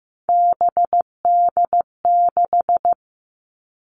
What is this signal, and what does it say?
Morse code, heard as a keyed tone, about 15 wpm, 700 Hz. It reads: BD6
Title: All Cut Numbers - 3-Digit 15wpm